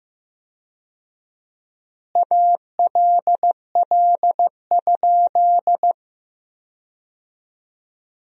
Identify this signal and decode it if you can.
Morse code, heard as a keyed tone, about 15 wpm, 700 Hz. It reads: ALL?
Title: 4x Word-Spacing - Sentences from Top 100 Words - 15wpm